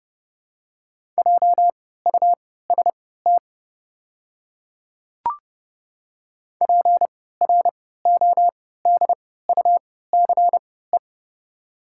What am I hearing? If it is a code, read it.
Morse code, heard as a keyed tone, about 30 wpm, spaced as for 17 wpm, 690 Hz. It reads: JUST PRODUCE
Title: 3x Character Spacing - Top 500 Words - 30wpm